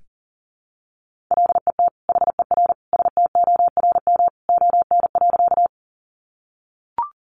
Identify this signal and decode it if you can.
Morse code, heard as a keyed tone, about 40 wpm, 700 Hz. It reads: LET HER STORM ON.
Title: Sentences from Top 1500 Words 40wpm